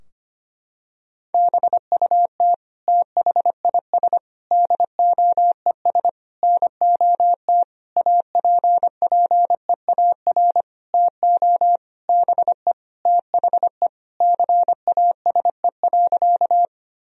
Morse code, heard as a keyed tone, about 25 wpm, 700 Hz. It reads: BUT THIS DOES NOT APPEAR TO BE THE CASE.
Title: Sentences from Top 1000 Words - Review Part 2 - 25wpm